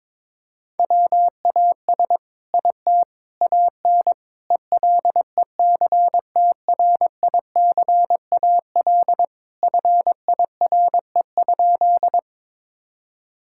Morse code, heard as a keyed tone, about 22 wpm, 700 Hz. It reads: WAS IT AN ELECTRICAL FIRE?
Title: Sentences from Top 1000 Words - Review Part 6 - 22wpm